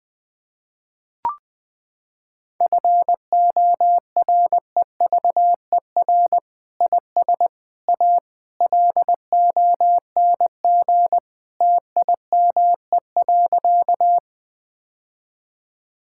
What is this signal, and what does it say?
Morse code, heard as a keyed tone, about 20 wpm, 700 Hz. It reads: FOREVER IS A LONG TIME.
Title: Sentences from Top 1600 Words 20wpm